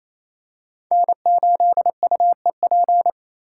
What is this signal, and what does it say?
Morse code, heard as a keyed tone, about 27 wpm, 700 Hz. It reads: N8UEP
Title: Code Groups - 5 Letter:Numbers - 28wpm